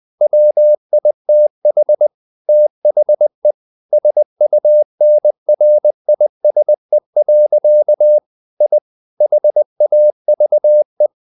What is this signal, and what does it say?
Morse code, heard as a keyed tone, about 20 wpm, 600 Hz. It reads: WITH THE SUNRISE. I HAVE
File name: the old man and the sea / the old man and the sea part 1 20wpm